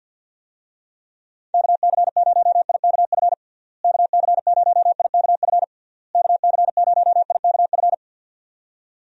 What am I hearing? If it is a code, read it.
Morse code, heard as a keyed tone, about 52 wpm, 700 Hz. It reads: KC0IKF KC0IKF KC0IKF
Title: USA Call Signs - 6 Characters - Repeated 3x - 50wpm